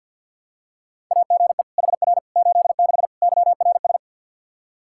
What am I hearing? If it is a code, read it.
Morse code, heard as a keyed tone, about 46 wpm, 700 Hz. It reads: AGE HR 86 YRS
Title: K1USN SST QSO Practice 50wpm